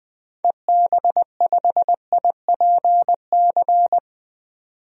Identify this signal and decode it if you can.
Morse code, heard as a keyed tone, about 21 wpm, 700 Hz. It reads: EB5IPC
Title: DXCC-100 Call Signs - 6 Characters - 20wpm